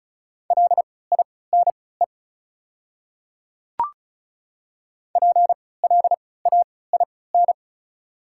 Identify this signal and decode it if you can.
Morse code, heard as a keyed tone, about 36 wpm, spaced as for 20 wpm, 700 Hz. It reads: LINE PLAIN
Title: 3x Character Spacing - Top 500 Words - 35wpm